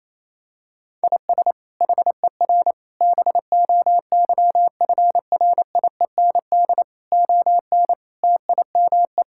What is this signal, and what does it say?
Morse code, heard as a keyed tone, about 28 wpm, 700 Hz. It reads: IS HER BOYFRIEND ON TIME
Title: Sentences from Top 1000 Words - Review Part 5 - 28wpm